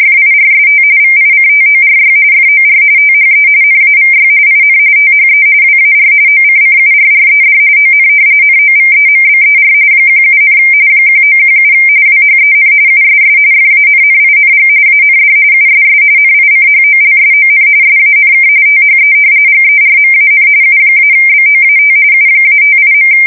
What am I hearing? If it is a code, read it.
RTTY, about 45.45 baud. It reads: FIRST VICTORY FOR THE NAVY IN THE WAR OCCURRED ON NOVEMBER 17, 1917 WHEN USS FANNING AND USS NICHOLSON SANK THE GERMAN U-BOAT U-58.
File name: fsk170 / usnavy4